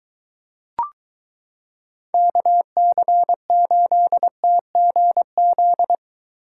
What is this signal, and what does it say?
Morse code, heard as a keyed tone, about 23 wpm, 700 Hz. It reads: KC8TGZ